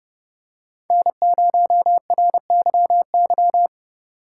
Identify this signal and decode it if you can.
Morse code, heard as a keyed tone, about 30 wpm, 700 Hz. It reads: N0RYY
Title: USA Call Signs - 5 Characters - 2x Character Spacing - 30wpm